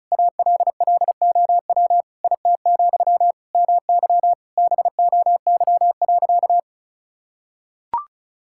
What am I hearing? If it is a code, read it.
Morse code, heard as a keyed tone, about 36 wpm, 700 Hz. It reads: ALLOW IT, MY BOY.
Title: Sentences from Top 1000 Words - Review Part 5 - 35wpm